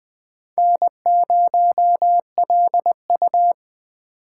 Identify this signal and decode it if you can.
Morse code, heard as a keyed tone, about 20 wpm, 700 Hz. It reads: N0LU